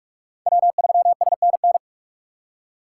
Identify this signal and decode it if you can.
Morse code, heard as a keyed tone, about 50 wpm, 700 Hz. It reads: W3SNN
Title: Speed Racing - USA 1x3 Call Signs 45wpm